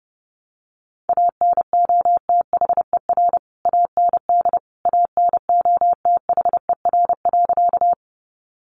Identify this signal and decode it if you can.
Morse code, heard as a keyed tone, about 28 wpm, 700 Hz. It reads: ANOTHER AND ANOTHER.
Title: Sentences from Top 600 Words 30wpm